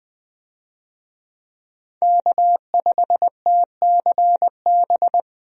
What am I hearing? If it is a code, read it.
Morse code, heard as a keyed tone, about 21 wpm, 700 Hz. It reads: K5TCB
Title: USA Call Signs - 5 Characters - Repeated 2x - 20wpm